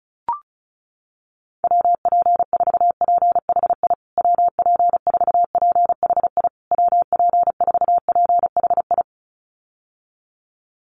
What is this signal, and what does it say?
Morse code, heard as a keyed tone, about 34 wpm, 700 Hz. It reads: WP4PHI WP4PHI WP4PHI
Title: USA Call Signs Repeated 3x 35wpm